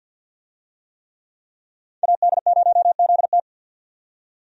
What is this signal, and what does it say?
Morse code, heard as a keyed tone, about 52 wpm, 700 Hz. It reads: AD0ZT
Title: USA Call Signs - 5 Characters - 50wpm